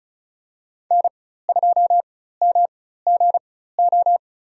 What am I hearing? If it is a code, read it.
Morse code, heard as a keyed tone, about 36 wpm, spaced as for 16 wpm, 700 Hz. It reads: N2MGO
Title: USA Call Signs - 5 Characters - 4x Character Spacing - 35wpm